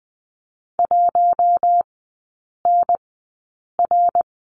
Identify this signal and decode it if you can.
Morse code, heard as a keyed tone, about 20 wpm, spaced as for 8 wpm, 700 Hz. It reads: 1NR
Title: 053 Lesson 10e - Letter r - Three Characters 20wpm